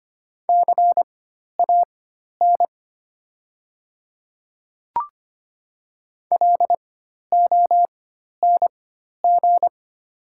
Morse code, heard as a keyed tone, about 25 wpm, spaced as for 12 wpm, 700 Hz. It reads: CAN LONG
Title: 4x Character Spacing - Top 100 Words - 25wpm